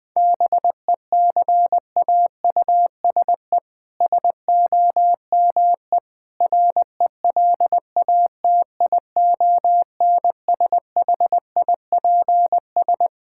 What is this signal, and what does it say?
Morse code, heard as a keyed tone, about 20 wpm, 700 Hz. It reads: BECAUSE SOME RELATIONSHIPS
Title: All-Age Appropriate Jokes - 20wpm